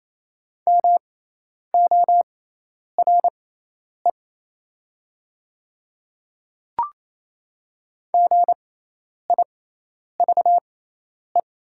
Morse code, heard as a keyed tone, about 28 wpm, spaced as for 10 wpm, 700 Hz. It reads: MORE GIVE
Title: 6x Character Spacing - Top 300 Words - 28wpm